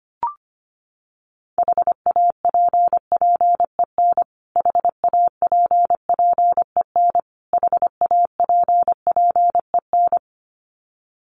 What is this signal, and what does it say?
Morse code, heard as a keyed tone, about 25 wpm, 700 Hz. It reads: HAPPEN HAPPEN HAPPEN
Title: Top 500 Words Repeated 25wpm